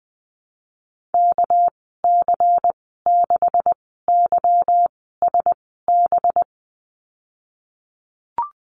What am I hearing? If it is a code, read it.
Morse code, heard as a keyed tone, about 19 wpm, spaced as for 14 wpm, 700 Hz. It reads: KC6YSB
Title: 245 Lesson 39g - Letter z - Call Signs 20wpm